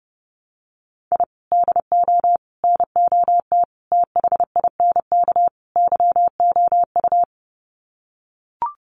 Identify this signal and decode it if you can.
Morse code, heard as a keyed tone, about 30 wpm, 700 Hz. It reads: I DO NOT THINK YOU
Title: Sets of 5 Words 30wpm